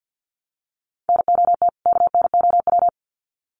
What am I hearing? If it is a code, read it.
Morse code, heard as a keyed tone, about 50 wpm, 700 Hz. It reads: NOT KNOW
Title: Sets of 2 Words 50wpm